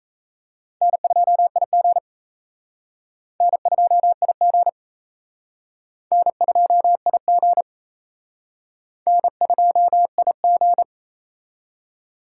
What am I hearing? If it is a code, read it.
Morse code, heard as a keyed tone, about 35 wpm, 700 Hz. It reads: N2IG N2IG N2IG N2IG
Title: Speed Racing - USA 1x2 Call Signs 28wpm